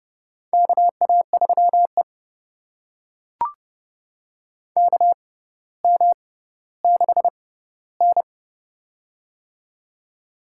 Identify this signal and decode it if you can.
Morse code, heard as a keyed tone, about 30 wpm, spaced as for 15 wpm, 700 Hz. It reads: KA3E KM6N
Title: USA Call Signs - 6x Character Spacing - 30wpm